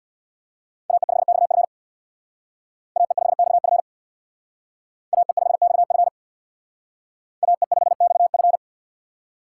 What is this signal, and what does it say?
Morse code, heard as a keyed tone, about 62 wpm, 700 Hz. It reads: AE5XF AE5XF AE5XF AE5XF
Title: Speed Racing - USA Call Signs 50wpm